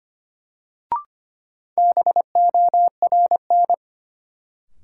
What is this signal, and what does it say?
Morse code, heard as a keyed tone, about 25 wpm, 700 Hz. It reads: BORN
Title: Sentences from Top 1000 Words - New Word Drill - 25wpm